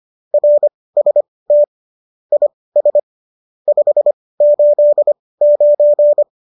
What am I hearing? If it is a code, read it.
Morse code, heard as a keyed tone, about 25 wpm, spaced as for 18 wpm, 590 Hz. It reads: RST IS 589
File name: qso204 18